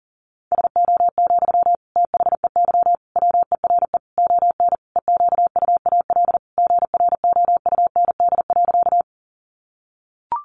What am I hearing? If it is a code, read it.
Morse code, heard as a keyed tone, about 40 wpm, 700 Hz. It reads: SO, THEY WERE ON EQUAL GROUND.
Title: Sentences from Top 1400 Words 40wpm